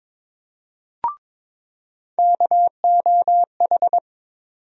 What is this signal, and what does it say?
Morse code, heard as a keyed tone, about 22 wpm, 700 Hz. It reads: KOH
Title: USA Call Sign Postfixes - 3 Letters 22wpm